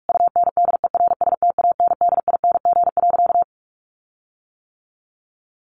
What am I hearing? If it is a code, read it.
Morse code, heard as a keyed tone, about 44 wpm, 700 Hz. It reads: UNDERSTANDING.
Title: Sentences from Top 1400 Words 45wpm